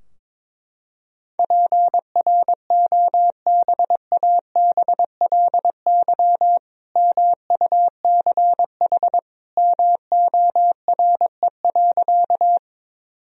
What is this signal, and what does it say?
Morse code, heard as a keyed tone, about 22 wpm, 700 Hz. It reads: PROBABLY MUCH MORE.
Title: Sentences from Top 1000 Words - Review Part 4 - 22wpm